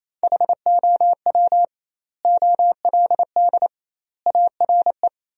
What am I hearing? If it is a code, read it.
Morse code, heard as a keyed tone, about 28 wpm, 700 Hz. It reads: HOW OLD ARE
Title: 2x Word-Spacing - Sentences from Top 200 Words - 28wpm